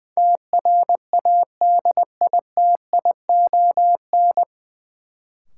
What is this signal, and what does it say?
Morse code, heard as a keyed tone, about 21 wpm, 700 Hz. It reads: TRADITION
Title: Sentences from Top 1500 Words - New Word Drill Alt - 20wpm